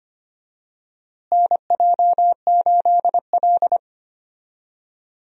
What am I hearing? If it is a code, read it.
Morse code, heard as a keyed tone, about 25 wpm, 700 Hz. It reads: NJ8L
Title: USA Call Signs - 4 Characters - Repeated 3x - 25wpm